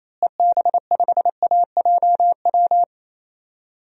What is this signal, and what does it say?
Morse code, heard as a keyed tone, about 28 wpm, 700 Hz. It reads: EB5AJW
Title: DXCC-100 Call Signs - 6 Characters - Repeated 3x - 28wpm